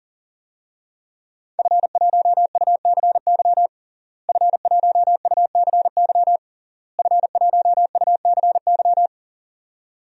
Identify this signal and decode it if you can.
Morse code, heard as a keyed tone, about 40 wpm, 700 Hz. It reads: F1UCY F1UCY F1UCY
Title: DXCC-100 Call Signs - 5 Characters - Repeated 3x - 40wpm